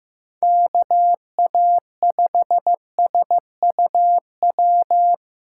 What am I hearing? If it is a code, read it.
Morse code, heard as a keyed tone, about 15 wpm, 700 Hz. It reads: KA5SUW
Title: USA Call Signs - 6x Character Spacing - 15wpm